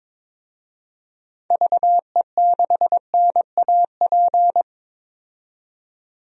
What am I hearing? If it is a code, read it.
Morse code, heard as a keyed tone, about 22 wpm, 700 Hz. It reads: VE6NAP
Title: DXCC-100 Call Signs - 6 Characters - 22wpm